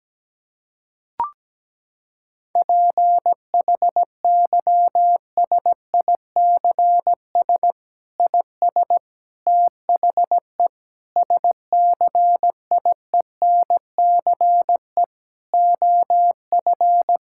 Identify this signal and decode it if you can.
Morse code, heard as a keyed tone, about 17 wpm, 700 Hz. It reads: PHYSICS IS THE SCIENCE OF
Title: Sentences from Top 1000 Words - Review Part 5 - 17wpm